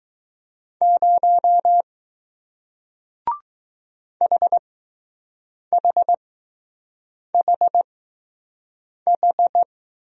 Morse code, heard as a keyed tone, about 20 wpm, spaced as for 6 wpm, 700 Hz. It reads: 0 HHHH